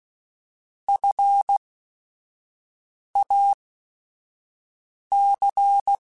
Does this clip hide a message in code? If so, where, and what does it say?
Yes: Morse code from 0.9 to 6.0 s, heard as a keyed tone, about 16 wpm, spaced as for 5 wpm, 800 Hz. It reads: FAC